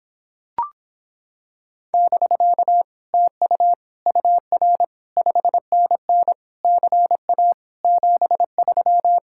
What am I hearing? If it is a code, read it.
Morse code, heard as a keyed tone, about 26 wpm, 700 Hz. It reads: <BK> TU UR 5NN CA 73